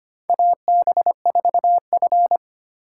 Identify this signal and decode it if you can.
Morse code, heard as a keyed tone, about 25 wpm, 700 Hz. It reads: AB4F
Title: USA Call Signs - 4 Characters - 3x Character Spacing - 25wpm